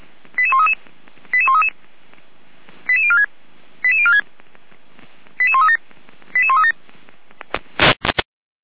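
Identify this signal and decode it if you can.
Five-tone selective call (ZVEI), heard as quick runs of five tones, tones of about 70 ms. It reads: 88133 88133 88046 88046 88137 88137
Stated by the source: ZVEI five-tone selective call received by SDR, archived by the Orlamünde fire brigade (feuerwehr-orlamuende).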